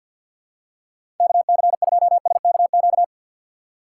Morse code, heard as a keyed tone, about 53 wpm, 700 Hz. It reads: KC2SKQ
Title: USA Call Signs - 6 Characters - 3x Character Spacing - 50wpm